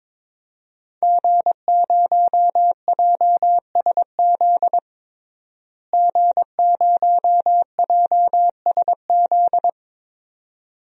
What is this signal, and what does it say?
Morse code, heard as a keyed tone, about 22 wpm, 700 Hz. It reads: G0JSZ G0JSZ
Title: DXCC-100 Call Signs - 5 Characters - Repeated 2x - 22wpm